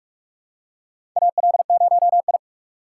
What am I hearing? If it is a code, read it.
Morse code, heard as a keyed tone, about 44 wpm, 700 Hz. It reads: AL0I
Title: USA Call Signs - 4 Characters - 4x Character Spacing - 45wpm